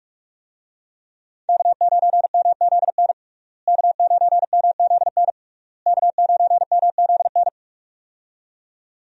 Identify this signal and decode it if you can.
Morse code, heard as a keyed tone, about 46 wpm, 700 Hz. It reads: K9MZN K9MZN K9MZN
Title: USA Call Signs - 5 Characters - Repeated 3x - 45wpm